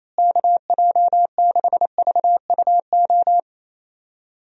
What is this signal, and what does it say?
Morse code, heard as a keyed tone, about 28 wpm, 700 Hz. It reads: KJ6VUO
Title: USA Call Signs - 6 Characters - Repeated 3x - 28wpm